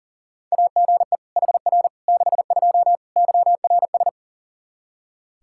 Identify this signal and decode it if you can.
Morse code, heard as a keyed tone, about 40 wpm, 700 Hz. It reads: AGE HR 62 YRS
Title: Practice QSO - Exchange 3 - More Info - 40wpm